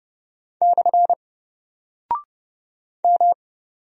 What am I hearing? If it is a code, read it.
Morse code, heard as a keyed tone, about 30 wpm, 700 Hz. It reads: / M
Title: Single Character (Letter:Number:Punctuation:Prosign) 30wpm